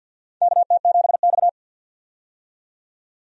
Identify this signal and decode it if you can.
Morse code, heard as a keyed tone, about 50 wpm, 700 Hz. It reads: KT7X
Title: USA Call Signs - 4 Characters - 50wpm